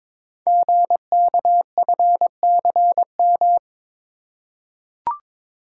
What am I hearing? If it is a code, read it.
Morse code, heard as a keyed tone, about 22 wpm, 700 Hz. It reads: GKFCM
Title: Code Groups - 5 Letters - 22wpm